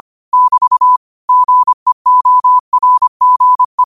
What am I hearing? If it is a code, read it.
Morse code, heard as a keyed tone, about 25 wpm, 1000 Hz. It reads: <BT> GEORGE
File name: qotdfiles / 25wpm